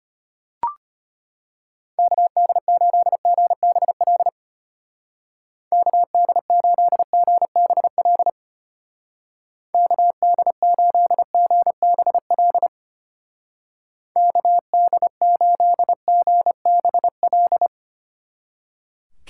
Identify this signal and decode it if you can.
Morse code, heard as a keyed tone, about 32 wpm, 700 Hz. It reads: KD8GBL KD8GBL KD8GBL KD8GBL